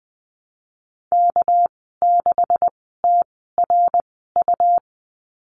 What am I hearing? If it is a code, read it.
Morse code, heard as a keyed tone, about 20 wpm, spaced as for 14 wpm, 700 Hz. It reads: K6TRU
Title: 213 Lesson 34g - Number 8 - Call Signs 20wpm